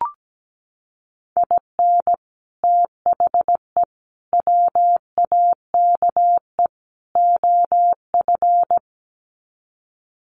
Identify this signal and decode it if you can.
Morse code, heard as a keyed tone, about 17 wpm, 700 Hz. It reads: IN THE WAKE OF
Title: Sets of 4 Words 17wpm